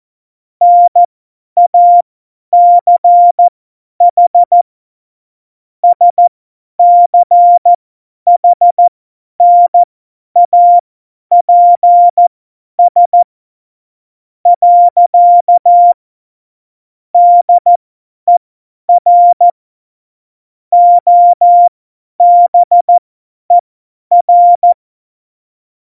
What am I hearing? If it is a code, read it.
Morse code, heard as a keyed tone, about 14 wpm, spaced as for 10 wpm, 700 Hz. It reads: NACH SCHNAPS. DER OBER
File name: Morsekurs / Witze / Witz 14. Januar 2020, 50 F70